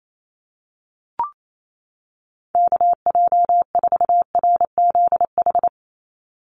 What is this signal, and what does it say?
Morse code, heard as a keyed tone, about 28 wpm, 700 Hz. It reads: KJ4RZH
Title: USA Call Signs 28wpm